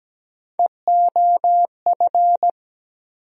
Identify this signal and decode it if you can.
Morse code, heard as a keyed tone, about 17 wpm, 700 Hz. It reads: EOF
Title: USA Call Sign Postfixes - 3 Letters 17wpm